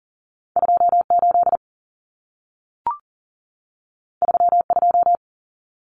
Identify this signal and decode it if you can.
Morse code, heard as a keyed tone, about 40 wpm, 700 Hz. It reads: 28 32
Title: Common Cut Numbers - 2-Digit 40wpm